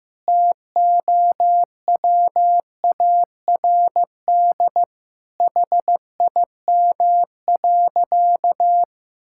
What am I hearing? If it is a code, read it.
Morse code, heard as a keyed tone, about 16 wpm, 700 Hz. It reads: TOWARD HIM.
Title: Sentences from Top 1000 Words - Review Part 6 - 15wpm